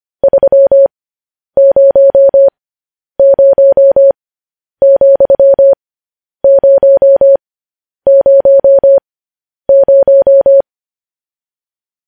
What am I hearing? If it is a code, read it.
Morse code, heard as a keyed tone, about 25 wpm, spaced as for 10 wpm, 550 Hz. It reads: 300,000
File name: podcast / News headlines-1x-25x10